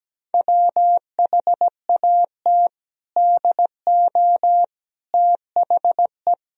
Morse code, heard as a keyed tone, about 17 wpm, 700 Hz. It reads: WHAT DO THE
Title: Sentences from Top 1000 Words - Review Part 3 - 17wpm